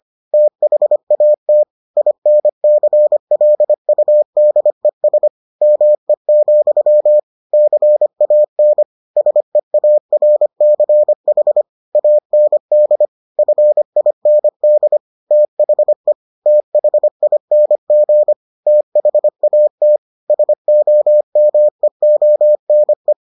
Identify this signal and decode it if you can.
Morse code, heard as a keyed tone, about 25 wpm, 600 Hz. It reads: THAT INCLUDES ME, CAN SEARCH AND FIND THE THING THAT SOMEONE